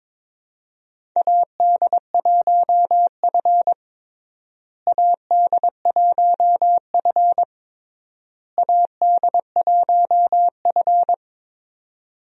Morse code, heard as a keyed tone, about 22 wpm, 700 Hz. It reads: AD1F AD1F AD1F
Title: USA Call Signs - 4 Characters - Repeated 3x - 22wpm